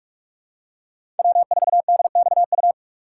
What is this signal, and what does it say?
Morse code, heard as a keyed tone, about 44 wpm, 700 Hz. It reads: W4DXU
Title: USA Call Signs - 5 Characters - 6x Character Spacing - 45wpm